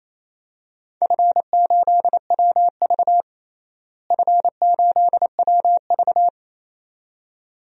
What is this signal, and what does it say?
Morse code, heard as a keyed tone, about 27 wpm, 700 Hz. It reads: F8WV F8WV
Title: DXCC-100 Call Signs - 4 Characters - Repeated 2x - 28wpm